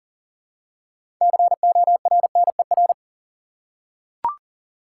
Morse code, heard as a keyed tone, about 40 wpm, 700 Hz. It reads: CORNER
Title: Six Letter Words - 40wpm